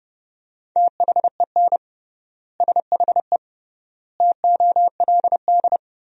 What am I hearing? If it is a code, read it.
Morse code, heard as a keyed tone, about 32 wpm, 700 Hz. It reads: THEN SHE TOLD
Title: 3x Word-Spacing - Sentences from Top 400 Words - 30wpm